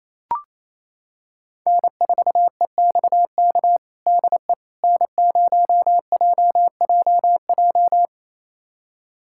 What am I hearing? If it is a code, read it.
Morse code, heard as a keyed tone, about 28 wpm, 700 Hz. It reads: N4EXK DE N0JJJ
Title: USA Callsign DE Callsign - 5 Characters - 28wpm